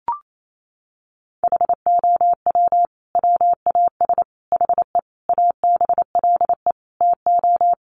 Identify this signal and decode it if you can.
Morse code, heard as a keyed tone, about 28 wpm, 700 Hz. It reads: HOW WAS HE ABLE TO
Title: Sentences from Top 1400 Words 28wpm